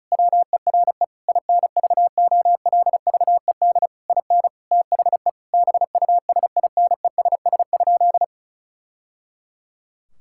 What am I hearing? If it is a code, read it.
Morse code, heard as a keyed tone, about 36 wpm, 700 Hz. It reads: WERE INVOLVED IN THE BUSINESS?
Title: Sentences from Top 1000 Words - Review Part 5 - 35wpm